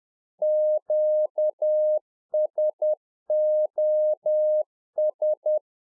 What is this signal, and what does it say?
Morse code, heard as a keyed tone, about 10 wpm, 610 Hz. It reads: QSOS
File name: 2011-04-mp3 / RICK JJ0MPI 10WPM